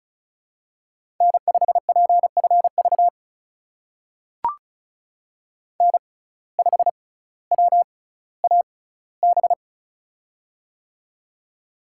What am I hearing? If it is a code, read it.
Morse code, heard as a keyed tone, about 36 wpm, spaced as for 18 wpm, 700 Hz. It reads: N5PFV N5WAB
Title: USA Call Signs - 5 Characters - 6x Character Spacing - 35wpm